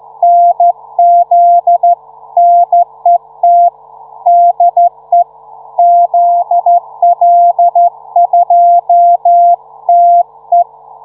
Morse code, heard as a keyed tone, about 13 wpm, 700 Hz. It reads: NZ NET DE ZL2TE